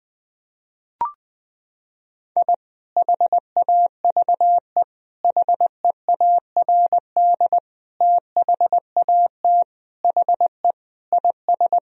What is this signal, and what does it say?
Morse code, heard as a keyed tone, about 21 wpm, 700 Hz. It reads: I HAVE HEARD THAT HE IS
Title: Sentences from Top 1000 Words - Review Part 3 - 20wpm